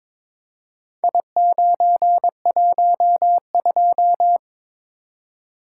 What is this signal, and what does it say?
Morse code, heard as a keyed tone, about 22 wpm, 700 Hz. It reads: I912
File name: Four Letter-Number Combinations - 22wpm